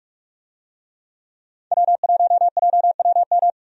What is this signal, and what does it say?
Morse code, heard as a keyed tone, about 42 wpm, 700 Hz. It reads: W1JWM
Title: USA Call Signs - 5 Characters - 45wpm